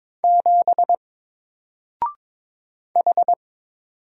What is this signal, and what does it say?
Morse code, heard as a keyed tone, about 22 wpm, 700 Hz. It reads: 7 H
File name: Single-Letter-Number-22wpm